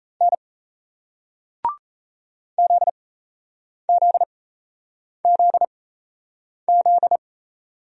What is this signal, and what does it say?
Morse code, heard as a keyed tone, about 38 wpm, spaced as for 8 wpm, 700 Hz. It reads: N ZZZZ